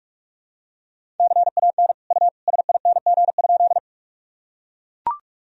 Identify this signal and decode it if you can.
Morse code, heard as a keyed tone, about 44 wpm, 700 Hz. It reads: CAN U SING?
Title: Sentences from Top 400 Words and Abbreviations 45wpm